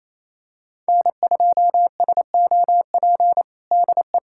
Morse code, heard as a keyed tone, about 27 wpm, 700 Hz. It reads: N2SOP DE
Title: USA Callsign DE Callsign - 5 Characters - 28wpm